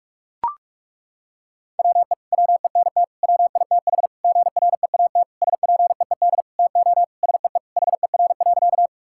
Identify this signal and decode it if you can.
Morse code, heard as a keyed tone, about 41 wpm, 700 Hz. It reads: WE WENT WITH GREAT SPEED TO SEE HER.
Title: Sentences from Top 1000 Words - Review Part 5 - 45wpm